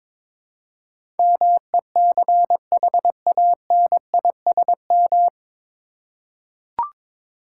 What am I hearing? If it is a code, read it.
Morse code, heard as a keyed tone, about 22 wpm, 700 Hz. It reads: MECHANISM
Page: Morse Code Ninja practice 22wpm